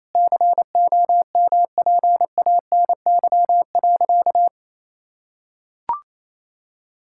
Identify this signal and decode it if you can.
Morse code, heard as a keyed tone, about 28 wpm, 700 Hz. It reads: COMPANY.
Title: Sentences from Top 1000 Words - Review Part 5 - 28wpm